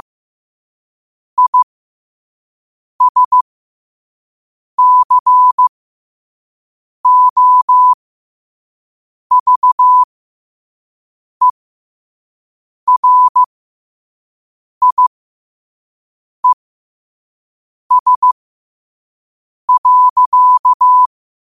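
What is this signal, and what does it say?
Morse code, heard as a keyed tone, about 15 wpm, spaced as for 5 wpm, 1000 Hz. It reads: ISCOVERIES.
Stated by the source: Morse code sent at 5 wpm.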